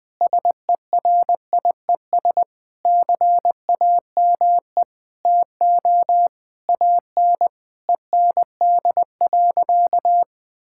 Morse code, heard as a keyed tone, about 20 wpm, 700 Hz. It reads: SERIES CAME TO AN END.
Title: Sentences from Top 1000 Words - Review Part 5 - 20wpm